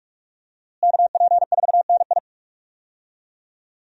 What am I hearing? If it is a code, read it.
Morse code, heard as a keyed tone, about 46 wpm, 700 Hz. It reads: KP4NI
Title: USA Call Signs - 5 Characters - Repeated 3x - 45wpm